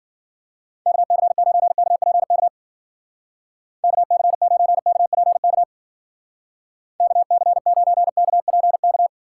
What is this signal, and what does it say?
Morse code, heard as a keyed tone, about 51 wpm, 700 Hz. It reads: KC9KPK KC9KPK KC9KPK